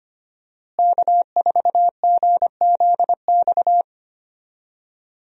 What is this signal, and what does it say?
Morse code, heard as a keyed tone, about 25 wpm, 700 Hz. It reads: K4GZX
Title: USA Call Signs - 5 Characters - Repeated 2x - 25wpm